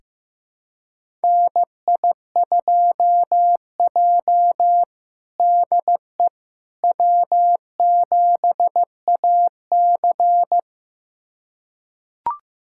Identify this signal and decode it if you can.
Morse code, heard as a keyed tone, about 15 wpm, 700 Hz. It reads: NI2J DE W7AC
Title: USA Callsign DE Callsign - 4 Characters - 15wpm